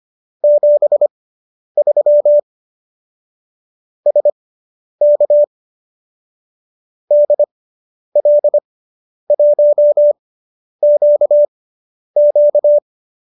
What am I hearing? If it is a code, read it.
Morse code, heard as a keyed tone, about 25 wpm, spaced as for 10 wpm, 590 Hz. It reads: 73 SK DL1QQ